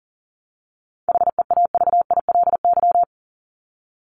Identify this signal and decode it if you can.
Morse code, heard as a keyed tone, about 40 wpm, 700 Hz. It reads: HEAVILY
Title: LY Suffix - Top 100 Words with Base Words 40wpm